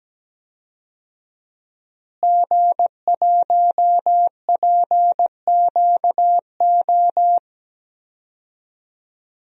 Morse code, heard as a keyed tone, about 17 wpm, 700 Hz. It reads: G1PQO
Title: DXCC-100 Call Signs - 5 Characters - Repeated 3x - 17wpm